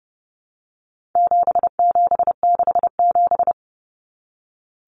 Morse code, heard as a keyed tone, about 30 wpm, 700 Hz. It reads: 7767
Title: Numbers - 4-Digit 30wpm